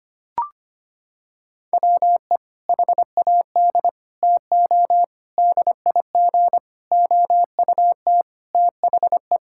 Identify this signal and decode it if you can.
Morse code, heard as a keyed tone, about 25 wpm, 700 Hz. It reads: WE HAD TO DIG OUT THE
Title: Sentences from Top 1600 Words 25wpm